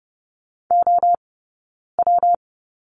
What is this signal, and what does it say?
Morse code, heard as a keyed tone, about 30 wpm, 700 Hz. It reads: O W
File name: Lesson 20d - Number 3 - Two Characters 30wpm